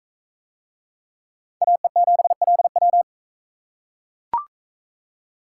Morse code, heard as a keyed tone, about 42 wpm, 700 Hz. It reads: AE7LW